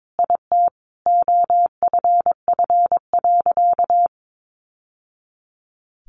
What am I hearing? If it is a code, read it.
Morse code, heard as a keyed tone, about 22 wpm, 700 Hz. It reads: IT OFF.
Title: Sentences from Top 1100 Words 22wpm